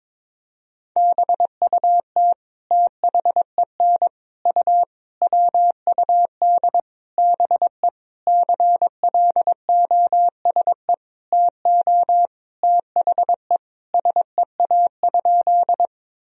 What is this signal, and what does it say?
Morse code, heard as a keyed tone, about 21 wpm, 700 Hz. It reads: BUT THEN U WUD BE CLOSE TO THE SEA?
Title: Sentences from Top 300 Words and Abbreviations 22wpm